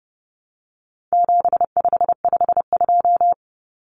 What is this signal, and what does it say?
Morse code, heard as a keyed tone, about 30 wpm, 700 Hz. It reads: 7552
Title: Common Cut Numbers - 4-Digit 30wpm